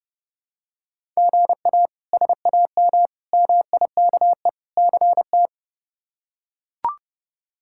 Morse code, heard as a keyed tone, about 30 wpm, 700 Hz. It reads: GA SAM MIKE CT
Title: K1USN SST QSO Practice - 30wpm